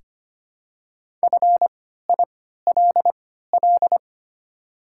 Morse code, heard as a keyed tone, about 25 wpm, spaced as for 14 wpm, 700 Hz. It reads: FILL
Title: 3x Character Spacing - Top 500 Words - 25wpm